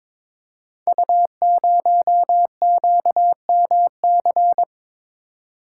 Morse code, heard as a keyed tone, about 22 wpm, 700 Hz. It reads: U0QMC
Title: Code Groups - 5 Letter:Numbers - 22wpm